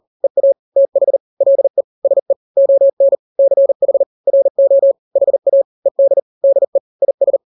Morse code, heard as a keyed tone, about 40 wpm, 550 Hz. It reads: EA TH LE SE ON CH RO HA ED DE IS
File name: n-grams / 2-grams 40 35 WPM